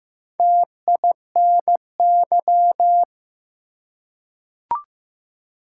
Morse code, heard as a keyed tone, about 15 wpm, 700 Hz. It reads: TINY
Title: Four Letter Words - 15wpm